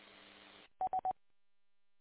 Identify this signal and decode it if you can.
There is no code.